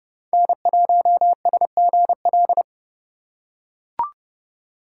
Morse code, heard as a keyed tone, about 30 wpm, 700 Hz. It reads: N1SGL
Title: USA Call Signs - 5 Characters - 3x Character Spacing - 30wpm